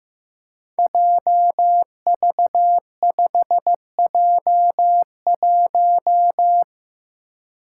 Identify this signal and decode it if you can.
Morse code, heard as a keyed tone, about 15 wpm, 700 Hz. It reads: JV5J1
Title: Five Challenging Letter-Number Combinations 15wpm